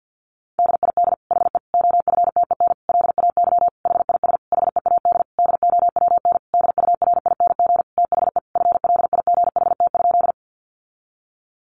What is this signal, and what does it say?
Morse code, heard as a keyed tone, about 49 wpm, 700 Hz. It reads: DID HE OFTEN LAY HIS HEAD DOWN DURING THE FLIGHT?
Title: Sentences from Top 1000 Words 50wpm